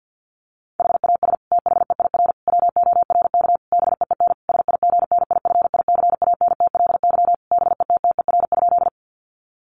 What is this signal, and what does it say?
Morse code, heard as a keyed tone, about 50 wpm, 700 Hz. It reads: HAS THEIR WORK BEEN SIGNIFICANTLY BETTER?
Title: Sentences from Top 1500 Words 50wpm